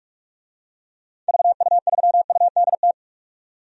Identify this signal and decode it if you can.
Morse code, heard as a keyed tone, about 47 wpm, 700 Hz. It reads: VU3UDT
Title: DXCC-100 Call Signs - 6 Characters - 45wpm